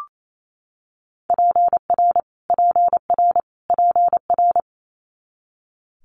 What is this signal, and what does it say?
Morse code, heard as a keyed tone, about 28 wpm, 700 Hz. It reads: PR PR PR
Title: Multi-letter Phonemes Repeated 3x 28wpm